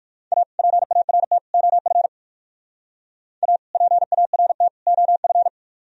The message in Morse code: A PART OF A PART OF

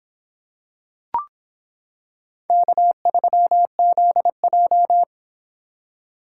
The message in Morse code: K3ZJ

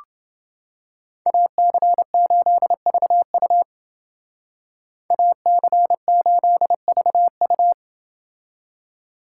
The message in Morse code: AC8VU AC8VU

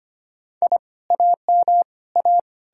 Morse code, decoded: I AM A

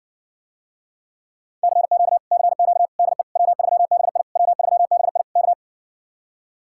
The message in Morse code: CQ CQ DE W3BI W3BI K

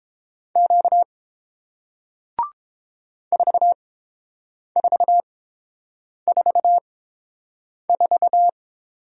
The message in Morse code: Q 4444

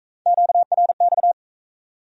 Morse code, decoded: QRX

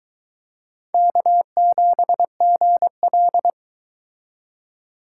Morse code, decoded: K7GL

